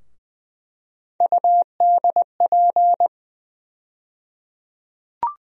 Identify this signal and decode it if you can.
Morse code, heard as a keyed tone, about 20 wpm, 700 Hz. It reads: UDP
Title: USA Call Sign Postfixes - 3 Letters 20wpm